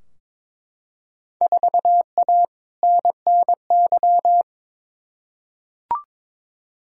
Morse code, heard as a keyed tone, about 21 wpm, 700 Hz. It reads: 4A NNY
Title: Field Day Exchanges Encore - 22wpm